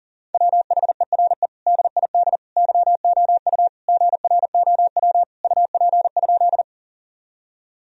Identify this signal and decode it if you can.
Morse code, heard as a keyed tone, about 40 wpm, 700 Hz. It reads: WHERE DID YOU GROW UP?
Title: Sentences from Top 1000 Words - Review Part 1 - 40wpm